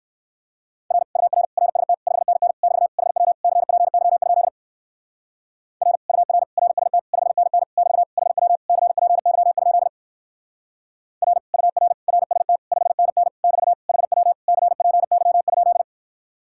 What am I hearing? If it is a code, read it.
Morse code, heard as a keyed tone, about 57 wpm, 700 Hz. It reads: R UR RST 5NN <BT> HW CPY? R UR RST 5NN <BT> HW CPY? R UR RST 5NN <BT> HW CPY?